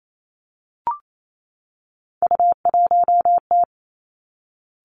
Morse code, heard as a keyed tone, about 28 wpm, 690 Hz. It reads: U1T
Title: Three Challenging Letter-Number Combinations 28wpm